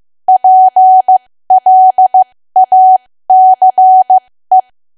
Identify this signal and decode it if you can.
Morse code, heard as a keyed tone, about 15 wpm, spaced as for 13 wpm, 750 Hz. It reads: PLACE
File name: cwa / W1AW-Level2 / 13WPM